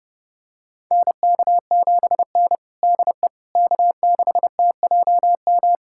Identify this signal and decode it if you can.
Morse code, heard as a keyed tone, about 30 wpm, 700 Hz. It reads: NK7N DE K6TJM
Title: USA Callsign DE Callsign 30wpm